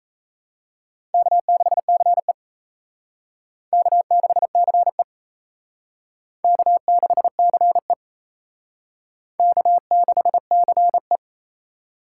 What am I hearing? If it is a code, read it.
Morse code, heard as a keyed tone, about 37 wpm, 700 Hz. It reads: K6CE K6CE K6CE K6CE